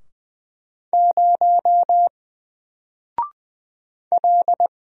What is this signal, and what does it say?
Morse code, heard as a keyed tone, about 20 wpm, 700 Hz. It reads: 0 L
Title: Single Letter-Number 20wpm